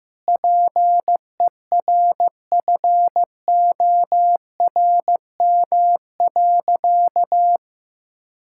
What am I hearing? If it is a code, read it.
Morse code, heard as a keyed tone, about 15 wpm, 700 Hz. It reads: PERFORM.
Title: Sentences from Top 1000 Words - Review Part 5 - 15wpm